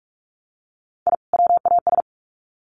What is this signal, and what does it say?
Morse code, heard as a keyed tone, about 44 wpm, 700 Hz. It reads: I WAS
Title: Sets of 2 Words 45wpm